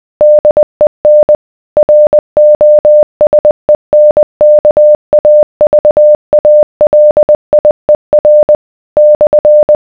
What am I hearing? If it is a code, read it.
Morse code, heard as a keyed tone, about 20 wpm, 600 Hz. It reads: DEN ROSENKAVALIER /